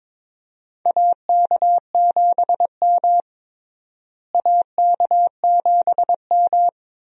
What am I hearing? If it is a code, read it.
Morse code, heard as a keyed tone, about 22 wpm, 700 Hz. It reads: AK7M AK7M
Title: USA Call Signs - 4 Characters - Repeated 2x - 22wpm